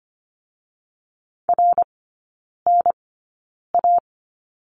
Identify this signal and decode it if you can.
Morse code, heard as a keyed tone, about 25 wpm, spaced as for 9 wpm, 700 Hz. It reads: RNA
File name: Lesson 17e - Letter u - Three Characters 25wpm